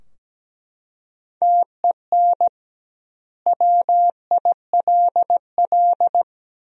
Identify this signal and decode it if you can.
Morse code, heard as a keyed tone, about 17 wpm, 700 Hz. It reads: TEN WILL
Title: 2x Word-Spacing - Sentences from Top 400 Words - 17wpm